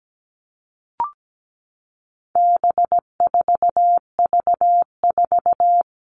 Morse code, heard as a keyed tone, about 17 wpm, 700 Hz. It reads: B4V4
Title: All Cut Numbers - 4-Digit 17wpm